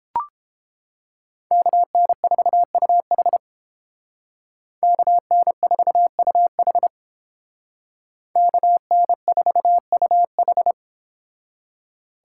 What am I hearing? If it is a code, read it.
Morse code, heard as a keyed tone, about 28 wpm, 700 Hz. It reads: KN4UH KN4UH KN4UH